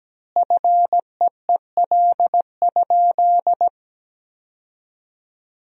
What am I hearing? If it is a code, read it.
Morse code, heard as a keyed tone, about 17 wpm, 700 Hz. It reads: FEEL?
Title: 4x Word-Spacing - Sentences from Top 400 Words - 17wpm